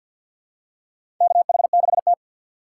Morse code, heard as a keyed tone, about 52 wpm, 700 Hz. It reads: KH6T
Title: USA Call Signs - 4 Characters - 3x Character Spacing - 50wpm